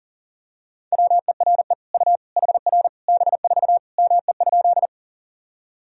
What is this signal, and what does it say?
Morse code, heard as a keyed tone, about 37 wpm, 700 Hz. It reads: WERE U HR B4 ME?
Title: Sentences from Top 200 Words and Abbreviations 40wpm